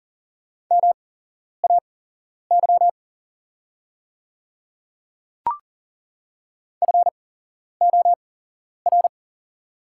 Morse code, heard as a keyed tone, about 42 wpm, spaced as for 11 wpm, 700 Hz. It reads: MAY FOR
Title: 8x Character Spacing - Top 200 Words - 40wpm